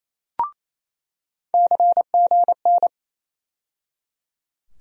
Morse code, heard as a keyed tone, about 28 wpm, 700 Hz. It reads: CGN
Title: USA Call Sign Postfixes - 3 Letters 28wpm